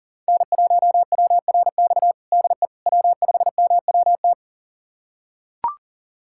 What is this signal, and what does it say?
Morse code, heard as a keyed tone, about 40 wpm, 700 Hz. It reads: N1WRX DE W5MWT
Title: USA Callsign DE Callsign - 5 Characters - 40wpm